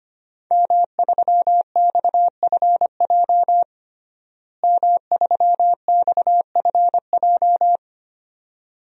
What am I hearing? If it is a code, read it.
Morse code, heard as a keyed tone, about 25 wpm, 700 Hz. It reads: M3XFJ M3XFJ
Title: DXCC-100 Call Signs - 5 Characters - Repeated 2x - 25wpm